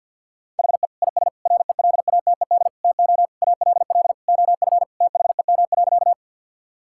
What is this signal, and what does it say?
Morse code, heard as a keyed tone, about 50 wpm, 700 Hz. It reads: HE IS RELATED TO ALL OF THEM.
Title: Sentences from Top 1000 Words - Review Part 5 - 50wpm